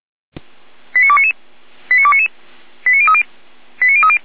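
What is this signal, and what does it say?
Five-tone selective call (ZVEI), heard as quick runs of five tones, tones of about 70 ms. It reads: 89299 89299 89030 89030
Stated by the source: ZVEI five-tone call 89174 received by SDR, from the feuerwehr-orlamuende.org (fire brigade Orlamünde) audio archive.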